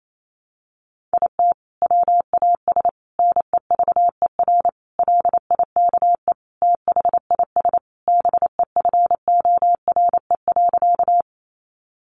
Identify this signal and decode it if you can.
Morse code, heard as a keyed tone, about 28 wpm, 700 Hz. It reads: IT WAS NEVER LIKE THIS BEFORE.
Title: A Sound Of Thunder 28wpm